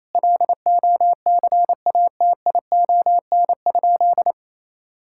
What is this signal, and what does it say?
Morse code, heard as a keyed tone, about 28 wpm, 700 Hz. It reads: LOCATION?
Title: Sentences from Top 1000 Words - Review Part 6 - 28wpm